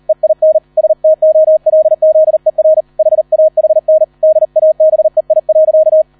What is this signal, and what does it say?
Morse code, heard as a keyed tone, about 38 wpm, 630 Hz. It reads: EIN STOLZER HAHN DABEI.